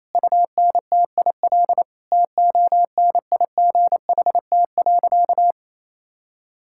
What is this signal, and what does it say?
Morse code, heard as a keyed tone, about 28 wpm, 700 Hz. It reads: UNTIL TONIGHT.
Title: Sentences from Top 1000 Words - Review Part 5 - 28wpm